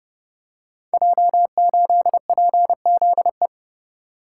Morse code, heard as a keyed tone, about 30 wpm, 700 Hz. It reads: J8PZE
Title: Code Groups - 5 Letter:Numbers - 30wpm